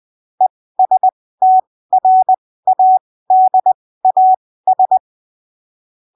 Morse code, heard as a keyed tone, about 20 wpm, spaced as for 15 wpm, 750 Hz. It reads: ESTRADAS
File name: arquivos / texto / texto 19 20ppm